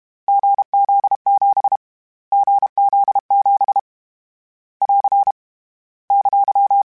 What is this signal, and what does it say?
Morse code, heard as a keyed tone, about 32 wpm, 800 Hz. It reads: GZ7 GZ7 <AR> !
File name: Difficult Combos / Difficult Combos-mp3s / GZ GZ7 @30